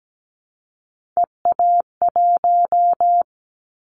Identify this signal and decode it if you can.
Morse code, heard as a keyed tone, about 17 wpm, 700 Hz. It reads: EA1
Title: Callsign Prefixes for Top 25 DXCC - 17wpm